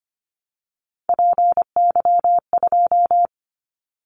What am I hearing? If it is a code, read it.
Morse code, heard as a keyed tone, about 25 wpm, 700 Hz. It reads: PY2
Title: Callsign Prefixes for Top 100 DXCC - 25wpm